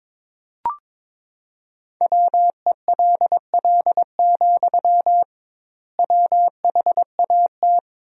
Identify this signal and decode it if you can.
Morse code, heard as a keyed tone, about 22 wpm, 700 Hz. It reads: WELL, WHAT
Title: 2x Word-Spacing - Sentences from Top 300 Words - 22wpm